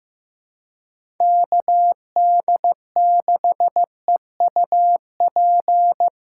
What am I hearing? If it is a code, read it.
Morse code, heard as a keyed tone, about 15 wpm, 700 Hz. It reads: KD6EUP
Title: USA Callsign DE Callsign - 6 Characters - 15wpm